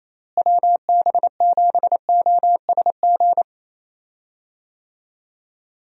Morse code, heard as a keyed tone, about 28 wpm, 700 Hz. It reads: WB7OSG
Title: USA Call Signs - 6 Characters - Repeated 3x - 28wpm